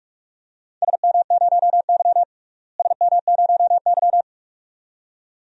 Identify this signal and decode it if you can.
Morse code, heard as a keyed tone, about 44 wpm, 700 Hz. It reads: SM0Y SM0Y
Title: DXCC-100 Call Signs - 4 Characters - Repeated 2x - 45wpm